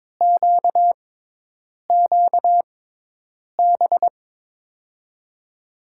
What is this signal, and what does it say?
Morse code, heard as a keyed tone, about 22 wpm, spaced as for 8 wpm, 700 Hz. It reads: QQB
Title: USA Call Signs - 6x Character Spacing - 22wpm